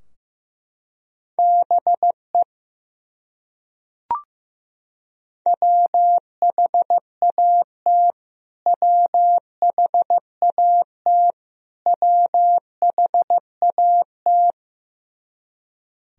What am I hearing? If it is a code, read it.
Morse code, heard as a keyed tone, about 16 wpm, 700 Hz. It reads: BE WHAT WHAT WHAT
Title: Top 100 Words Repeated 15wpm